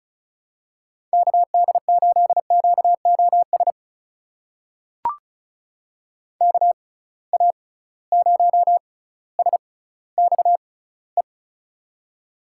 Morse code, heard as a keyed tone, about 36 wpm, spaced as for 18 wpm, 700 Hz. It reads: KD8QOS KA0SXE
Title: USA Call Signs - 6 Characters - 6x Character Spacing - 35wpm